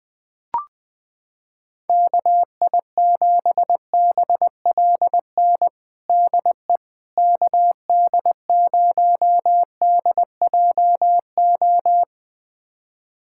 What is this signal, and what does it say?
Morse code, heard as a keyed tone, about 20 wpm, 700 Hz. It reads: KI7BLN DE KD0DJO
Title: USA Callsign DE Callsign 20wpm